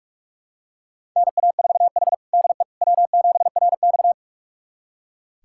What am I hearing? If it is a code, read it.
Morse code, heard as a keyed tone, about 44 wpm, 700 Hz. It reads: NA4H DE W7RX
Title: USA Callsign DE Callsign - 4 Characters - 45wpm